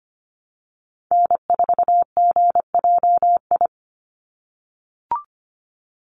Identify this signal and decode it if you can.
Morse code, heard as a keyed tone, about 25 wpm, 700 Hz. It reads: N4GJI
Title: USA Call Signs 25wpm